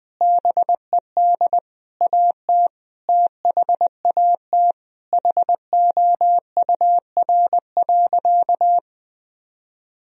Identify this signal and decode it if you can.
Morse code, heard as a keyed tone, about 21 wpm, 700 Hz. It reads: BED AT THAT HOUR.
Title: Sentences from Top 1000 Words - Review Part 3 - 20wpm